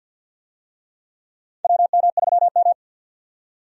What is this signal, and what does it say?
Morse code, heard as a keyed tone, about 49 wpm, 700 Hz. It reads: WM3M
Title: USA Call Signs - 4 Characters - 50wpm